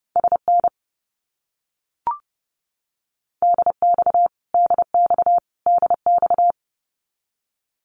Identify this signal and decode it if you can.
Morse code, heard as a keyed tone, about 30 wpm, 700 Hz. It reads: SN DX DX DX
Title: Abbreviations Repeated 3x - 30wpm